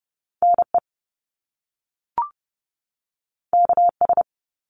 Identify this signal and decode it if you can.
Morse code, heard as a keyed tone, about 30 wpm, 700 Hz. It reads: NE KS
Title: US States 30wpm